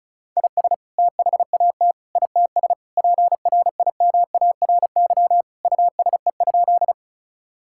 Morse code, heard as a keyed tone, about 34 wpm, 700 Hz. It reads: IS THAT ITS PRIMARY USE?
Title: Sentences from Top 1000 Words - Review Part 5 - 35wpm